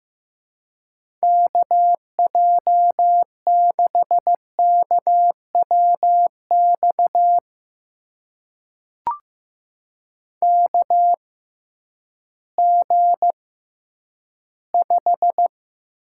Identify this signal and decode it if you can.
Morse code, heard as a keyed tone, about 15 wpm, spaced as for 10 wpm, 700 Hz. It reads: KJ6KWX KG5
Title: USA Call Signs - 6 Characters - 6x Character Spacing - 15wpm